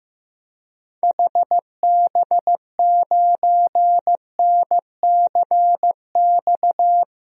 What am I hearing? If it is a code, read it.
Morse code, heard as a keyed tone, about 15 wpm, 700 Hz. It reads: HB9NCX
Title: DXCC-100 Call Signs - 6 Characters - Repeated 2x - 15wpm